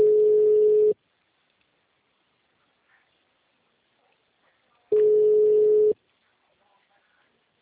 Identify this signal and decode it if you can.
Ringback tone (Europe, 425 Hz).
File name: out- -2026- - - .27